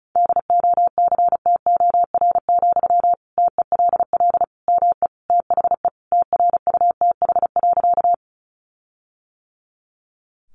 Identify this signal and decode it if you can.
Morse code, heard as a keyed tone, about 34 wpm, 700 Hz. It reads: DOCTOR, TELL ME THE TRUTH.
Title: Sentences from Top 1500 Words 35wpm